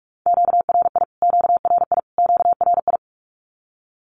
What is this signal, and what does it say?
Morse code, heard as a keyed tone, about 44 wpm, 700 Hz. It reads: QRI QRI QRI
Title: Q-Codes Repeated 3x 45wpm